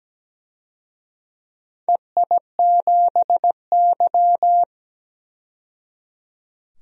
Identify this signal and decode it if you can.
Morse code, heard as a keyed tone, about 17 wpm, 700 Hz. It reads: EI7Y
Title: DXCC-100 Call Signs - 4 Characters - 17wpm